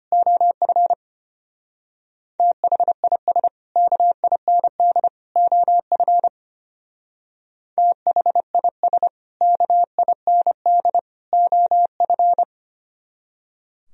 Morse code, heard as a keyed tone, about 33 wpm, 700 Hz. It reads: OF THIS KIND OF THIS KIND OF